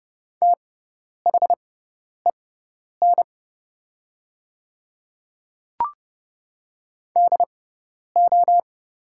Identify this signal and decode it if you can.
Morse code, heard as a keyed tone, about 32 wpm, spaced as for 10 wpm, 700 Hz. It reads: THEN DO
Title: 6x Character Spacing - Top 500 Words - 30wpm